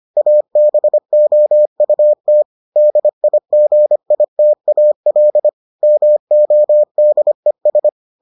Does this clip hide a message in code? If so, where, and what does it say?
Yes: Morse code from 0.2 to 7.9 s, heard as a keyed tone, about 25 wpm, 600 Hz. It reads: ABOUT DIGITAL MODES